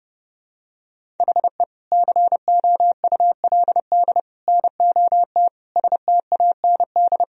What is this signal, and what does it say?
Morse code, heard as a keyed tone, about 30 wpm, 700 Hz. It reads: HE COULD NOT STAND
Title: Sentences from Top 1000 Words - Review Part 1 - 30wpm